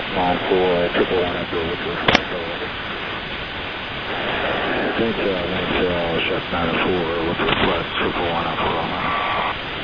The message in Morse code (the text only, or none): none